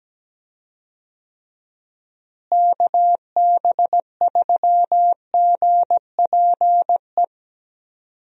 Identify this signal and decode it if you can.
Morse code, heard as a keyed tone, about 17 wpm, 700 Hz. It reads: KB3GPE
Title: USA Call Signs - 6 Characters - Repeated 2x - 17wpm